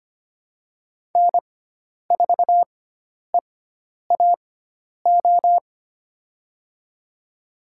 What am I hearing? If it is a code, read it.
Morse code, heard as a keyed tone, about 25 wpm, spaced as for 10 wpm, 700 Hz. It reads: N4EAO
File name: Lesson 16g - Letter c - Call Signs 25wpm